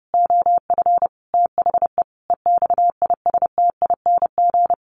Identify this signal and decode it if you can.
Morse code, heard as a keyed tone, about 28 wpm, 700 Hz. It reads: OF THE EXISTING
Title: Sentences from Top 1500 Words 30wpm